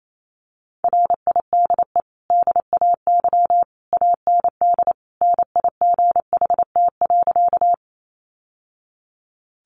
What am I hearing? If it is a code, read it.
Morse code, heard as a keyed tone, about 28 wpm, 700 Hz. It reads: RIDE DAY AND NIGHT.
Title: Sentences from Top 1200 Words 28wpm